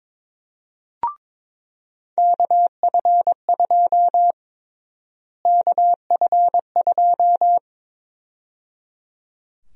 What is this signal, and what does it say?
Morse code, heard as a keyed tone, about 22 wpm, 700 Hz. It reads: KF2 KF2
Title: USA Call Sign Prefixes - Repeated 2x - 22wpm